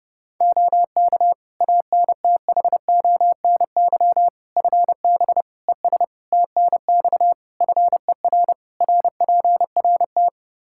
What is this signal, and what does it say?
Morse code, heard as a keyed tone, about 31 wpm, 700 Hz. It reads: OK ANTHONY FB ES TNX FER RPRT
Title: Practice QSO - Exchange 2 - Info - 30wpm